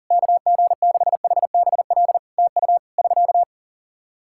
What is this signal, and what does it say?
Morse code, heard as a keyed tone, about 40 wpm, 700 Hz. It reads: KG6HBL TU <SK>
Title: Practice QSO - Exchange 4 - Ending - 40wpm